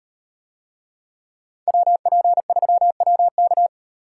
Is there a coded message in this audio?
Yes: Morse code from 1.7 to 3.7 s, heard as a keyed tone, about 38 wpm, 700 Hz. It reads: WP3WK